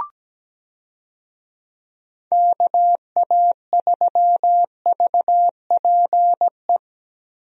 Morse code, heard as a keyed tone, about 17 wpm, 700 Hz. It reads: KA3VPE
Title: USA Call Signs - 6 Characters - Repeated 3x - 17wpm